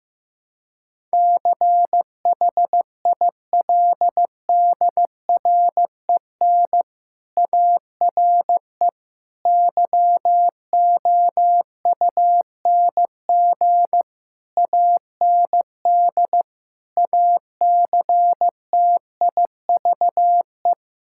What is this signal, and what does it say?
Morse code, heard as a keyed tone, about 15 wpm, 700 Hz. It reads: CHILDREN ARE YOUNG AND ACTIVE
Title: Sentences from Top 1000 Words - Review Part 6 - 15wpm